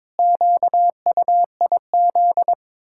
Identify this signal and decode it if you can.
Morse code, heard as a keyed tone, about 22 wpm, 700 Hz. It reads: QUIZ
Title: Pangram Challenge v2 - 22wpm